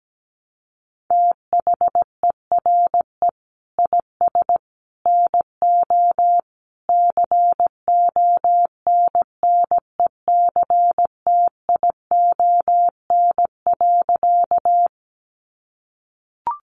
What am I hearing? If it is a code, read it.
Morse code, heard as a keyed tone, about 17 wpm, 700 Hz. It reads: THERE IS NO CONNECTION.
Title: Sentences from Top 1300 Words 17wpm